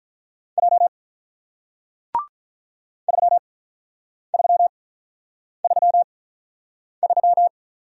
Morse code, heard as a keyed tone, about 47 wpm, spaced as for 9 wpm, 700 Hz. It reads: J 3333